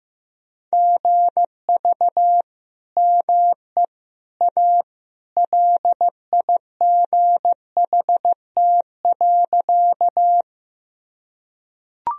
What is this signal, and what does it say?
Morse code, heard as a keyed tone, about 15 wpm, 700 Hz. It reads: GV ME A LIGHT.
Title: Sentences from Top 200 Words and Abbreviations 15wpm